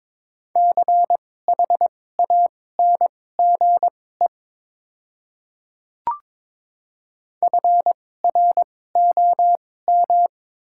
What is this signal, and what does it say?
Morse code, heard as a keyed tone, about 21 wpm, spaced as for 16 wpm, 700 Hz. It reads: CHANGE FROM
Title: 2x Character Spacing - Top 200 Words - 22wpm